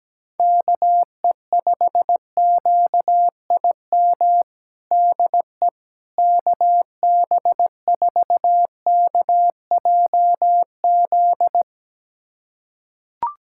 Morse code, heard as a keyed tone, about 17 wpm, 700 Hz. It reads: KE5QIM DE KB4KJZ
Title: USA Callsign DE Callsign - 6 Characters - 17wpm